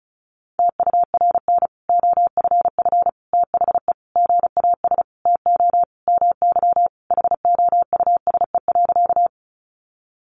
Morse code, heard as a keyed tone, about 34 wpm, 700 Hz. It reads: TURN OFF THE GAS TO MY HOUSE.
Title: Sentences from Top 1100 Words 35wpm